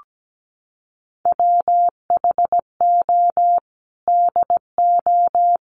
Morse code, heard as a keyed tone, about 17 wpm, 700 Hz. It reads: WHO DO
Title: Sentences from Top 1100 Words 17wpm